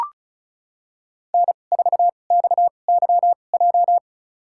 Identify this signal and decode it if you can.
Morse code, heard as a keyed tone, about 36 wpm, spaced as for 25 wpm, 700 Hz. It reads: N4XYJ
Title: USA Call Signs - 5 Characters - 2x Character Spacing - 35wpm